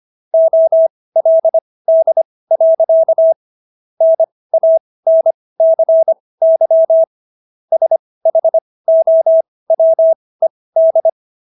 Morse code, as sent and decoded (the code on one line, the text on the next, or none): --- .-.. -.. .-.-.- / -. .- -. -.-. -.-- / ... .... --- .-- . -..
OLD. NANCY SHOWED